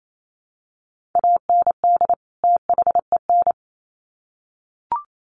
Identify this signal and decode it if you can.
Morse code, heard as a keyed tone, about 28 wpm, 700 Hz. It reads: AND THEN
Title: Sets of 2 Words 28wpm